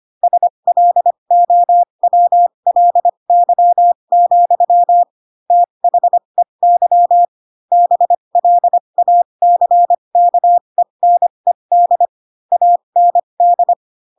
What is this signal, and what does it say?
Morse code, heard as a keyed tone, about 25 wpm, spaced as for 22 wpm, 700 Hz. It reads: SLOWLY, THEY BLACKENED AND
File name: Pinocchio / Pinocchio ch6 18 25 ik1hge